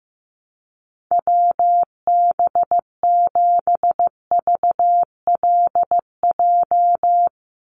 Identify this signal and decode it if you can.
Morse code, heard as a keyed tone, about 15 wpm, 700 Hz. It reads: WB7VLJ
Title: USA Call Signs 15wpm